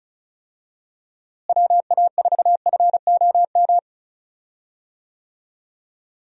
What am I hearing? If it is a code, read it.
Morse code, heard as a keyed tone, about 36 wpm, 700 Hz. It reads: WA4FOM
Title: USA Call Signs - 6 Characters - Repeated 3x - 35wpm